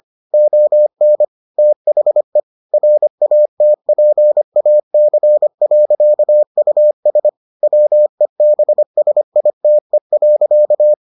Morse code, heard as a keyed tone, about 25 wpm, 600 Hz. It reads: ON THE RATPAC.US WEBSITE.